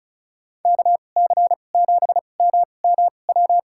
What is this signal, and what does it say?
Morse code, heard as a keyed tone, about 36 wpm, spaced as for 25 wpm, 700 Hz. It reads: KC7MMW
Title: USA Call Signs - 6 Characters - 2x Character Spacing - 35wpm